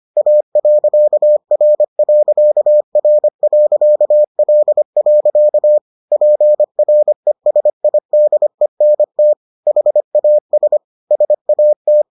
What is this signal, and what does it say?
Morse code, heard as a keyed tone, about 25 wpm, 600 Hz. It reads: A.R.R.L. PRESIDENT HAS SAT